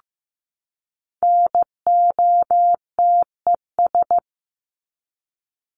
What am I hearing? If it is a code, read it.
Morse code, heard as a keyed tone, about 15 wpm, 700 Hz. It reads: NOTES
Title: ES:S Suffix - Top 100 Words with Base Words 15wpm